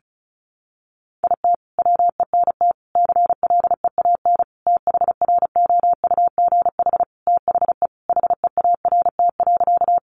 Morse code, heard as a keyed tone, about 34 wpm, 700 Hz. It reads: IT WENT CLEAN THROUGH THE HEART.
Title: Sentences from Top 1200 Words 35wpm